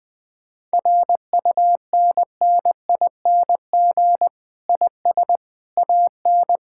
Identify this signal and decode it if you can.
Morse code, heard as a keyed tone, about 20 wpm, 700 Hz. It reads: RUNNING IS AN
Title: Difficult Sentences with Hints - 20wpm